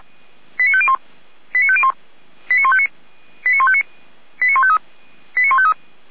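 Five-tone selective call (ZVEI), heard as quick runs of five tones, tones of about 70 ms. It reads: 89501 89501 89159 89159 89153 89153
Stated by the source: ZVEI five-tone selective call received on an SDR.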